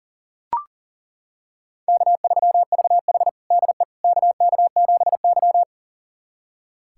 Morse code, heard as a keyed tone, about 40 wpm, 700 Hz. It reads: K3VH DE KK7Y
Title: USA Callsign DE Callsign - 4 Characters - 40wpm